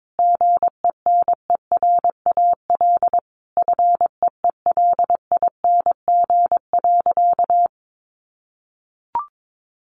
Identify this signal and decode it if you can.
Morse code, heard as a keyed tone, about 22 wpm, 700 Hz. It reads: GENERAL FEELING.
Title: Sentences from Top 1400 Words 22wpm